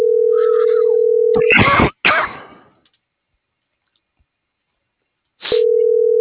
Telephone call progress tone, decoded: ringback tone

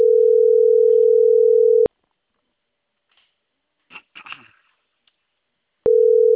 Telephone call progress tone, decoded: ringback tone

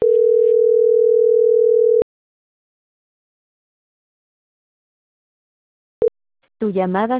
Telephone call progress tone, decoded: ringback tone

